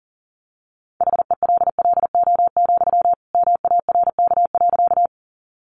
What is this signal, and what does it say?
Morse code, heard as a keyed tone, about 40 wpm, 700 Hz. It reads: HELLO, MARK.
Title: Sentences from Top 1100 Words 40wpm